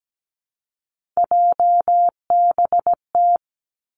Morse code, heard as a keyed tone, about 17 wpm, 700 Hz. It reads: JBT
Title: Three Challenging Letter-Number Combinations 17wpm